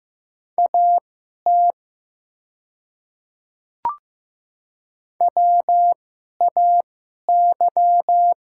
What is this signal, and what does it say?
Morse code, heard as a keyed tone, about 15 wpm, spaced as for 11 wpm, 700 Hz. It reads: AT WAY
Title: 2x Character Spacing - Top 100 Words - 15wpm